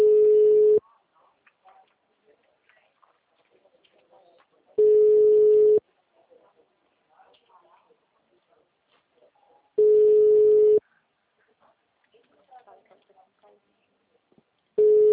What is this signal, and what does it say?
Ringback tone (Europe, 425 Hz), with speech.